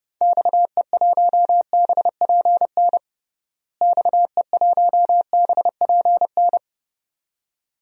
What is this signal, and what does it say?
Morse code, heard as a keyed tone, about 30 wpm, 700 Hz. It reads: XE1BPN XE1BPN
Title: DXCC-100 Call Signs - 6 Characters - Repeated 2x - 30wpm